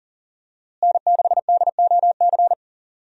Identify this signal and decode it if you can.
Morse code, heard as a keyed tone, about 40 wpm, 700 Hz. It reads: N6DOC
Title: USA Call Signs - 5 Characters - Repeated 2x - 40wpm